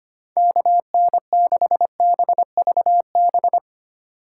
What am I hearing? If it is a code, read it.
Morse code, heard as a keyed tone, about 25 wpm, 700 Hz. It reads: KN6BVB
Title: USA Call Signs - 6 Characters - 2x Character Spacing - 25wpm